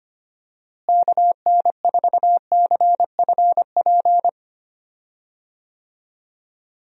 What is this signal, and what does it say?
Morse code, heard as a keyed tone, about 25 wpm, 700 Hz. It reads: KN4CFP
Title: USA Call Signs - 6 Characters - Repeated 3x - 25wpm